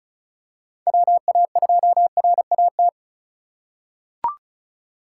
Morse code, heard as a keyed tone, about 36 wpm, 700 Hz. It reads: WA2RAT
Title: USA Call Signs - 6 Characters - 3x Character Spacing - 35wpm